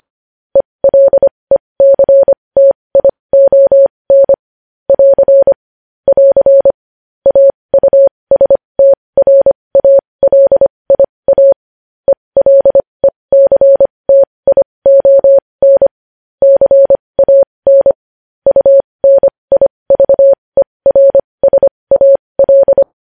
Morse code, heard as a keyed tone, about 25 wpm, spaced as for 20 wpm, 550 Hz. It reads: ELECTION <AR> <AR> AUSTRALIA ELECTION CAN UNIVERSAL
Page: news headlines in morse code at 20 wpm